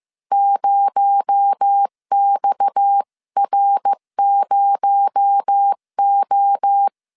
Morse code, heard as a keyed tone, about 15 wpm, 800 Hz. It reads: 0XR0O
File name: CW / DET / 1 9 1500C S A 14wpm 3222